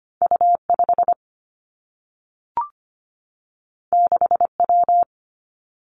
Morse code, heard as a keyed tone, about 25 wpm, 700 Hz. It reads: U5 6W
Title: Two Challenging Letter-Number Combinations 25wpm